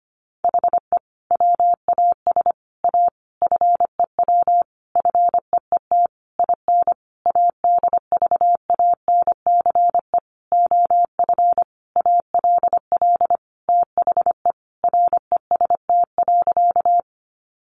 Morse code, heard as a keyed tone, about 25 wpm, 700 Hz. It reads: HE WAS A FEW FEET IN ADVANCE OF ALL THE REST.
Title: Sentences from Top 1400 Words 25wpm